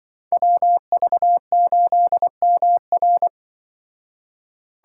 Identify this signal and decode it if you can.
Morse code, heard as a keyed tone, about 23 wpm, 700 Hz. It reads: WV8MR
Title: Speed Racing - USA 2x2 Call Signs 20wpm